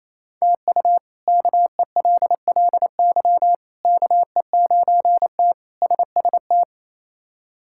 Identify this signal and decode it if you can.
Morse code, heard as a keyed tone, about 28 wpm, 700 Hz. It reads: TU KELLY KE9T SST
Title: K1USN SST QSO Practice - 28wpm